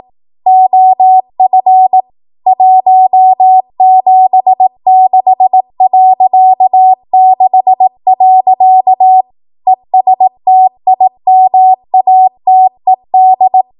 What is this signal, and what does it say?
Morse code, heard as a keyed tone, about 18 wpm, 750 Hz. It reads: OF 176.6. ESTIMATED
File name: Archive / 18 WPM / 18WPM